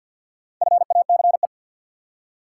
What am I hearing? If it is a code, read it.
Morse code, heard as a keyed tone, about 52 wpm, 700 Hz. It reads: FACE
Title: Four Letter Words - 50wpm